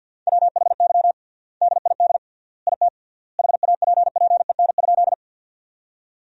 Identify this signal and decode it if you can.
Morse code, heard as a keyed tone, about 50 wpm, 700 Hz. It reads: WHY DID IT HAPPEN?
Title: 3x Word-Spacing - Sentences from Top 400 Words - 50wpm